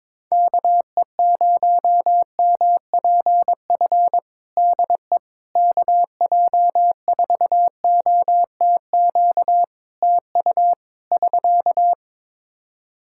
Morse code, heard as a keyed tone, about 22 wpm, 700 Hz. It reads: KE0MPF DE KJ4OTQ TU <SK>
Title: Practice QSO - Exchange 4 - Ending - 22wpm